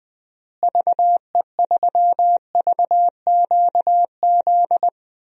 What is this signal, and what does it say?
Morse code, heard as a keyed tone, about 21 wpm, 700 Hz. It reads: VE3VQZ
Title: DXCC-100 Call Signs - 6 Characters - Repeated 2x - 20wpm